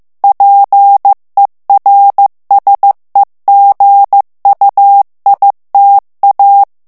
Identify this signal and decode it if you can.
Morse code, heard as a keyed tone, about 15 wpm, 800 Hz. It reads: PERSEGUITA
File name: ITALIEN texte clair 1 vitesse 14 mots minute